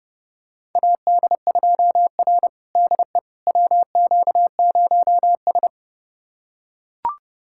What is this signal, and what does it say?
Morse code, heard as a keyed tone, about 31 wpm, 700 Hz. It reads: AD2R DE WQ0S
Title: USA Callsign DE Callsign - 4 Characters - 30wpm